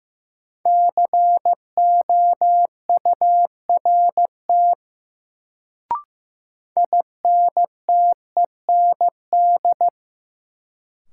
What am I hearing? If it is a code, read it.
Morse code, heard as a keyed tone, about 15 wpm, 700 Hz. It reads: COURT INTEND
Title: Sentences from Top 1600 Words - New Word Drill - 15wpm